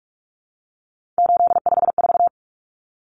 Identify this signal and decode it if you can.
Morse code, heard as a keyed tone, about 44 wpm, 700 Hz. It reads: 854
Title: Common Cut Numbers - 3-Digit 45wpm